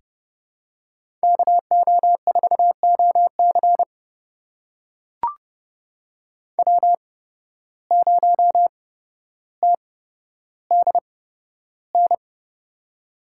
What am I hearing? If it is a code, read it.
Morse code, heard as a keyed tone, about 32 wpm, spaced as for 13 wpm, 700 Hz. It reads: KO4OC W0TDN
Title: USA Call Signs - 5 Characters - 8x Character Spacing - 30wpm